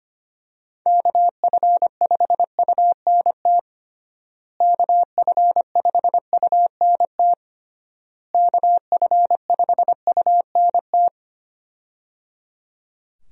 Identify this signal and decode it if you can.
Morse code, heard as a keyed tone, about 25 wpm, 700 Hz. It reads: KF5UNT KF5UNT KF5UNT
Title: USA Call Signs - 6 Characters - Repeated 3x - 25wpm